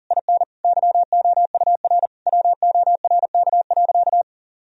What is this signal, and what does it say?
Morse code, heard as a keyed tone, about 40 wpm, 700 Hz. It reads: IN YOUR WORK.
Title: Sentences from Top 1000 Words - Review Part 4 - 40wpm